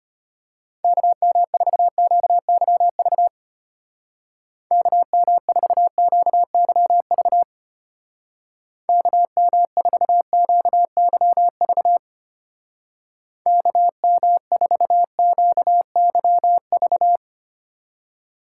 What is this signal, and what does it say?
Morse code, heard as a keyed tone, about 31 wpm, 700 Hz. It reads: KM4QYV KM4QYV KM4QYV KM4QYV